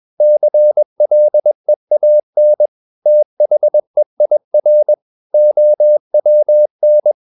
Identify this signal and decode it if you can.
Morse code, heard as a keyed tone, about 21 wpm, 600 Hz. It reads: CLEAN THEIR OWN
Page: CW med Gnister